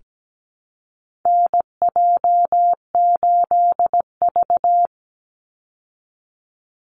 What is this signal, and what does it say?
Morse code, heard as a keyed tone, about 17 wpm, 700 Hz. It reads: NJ8V
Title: Call Signs Repeated 2x - 17wpm